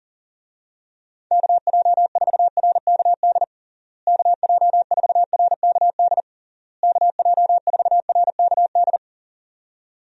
Morse code, heard as a keyed tone, about 40 wpm, 700 Hz. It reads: KJ4RKD KJ4RKD KJ4RKD
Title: USA Call Signs - 6 Characters - Repeated 3x - 40wpm